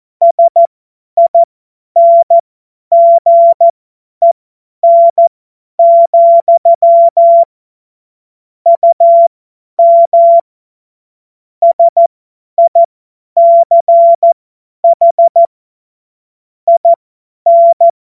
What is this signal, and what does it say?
Morse code, heard as a keyed tone, about 14 wpm, spaced as for 10 wpm, 680 Hz. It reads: SINGEN, UM SICH IN
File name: Morsekurs / Witze / Witz 22. Juni 2021, 50 F70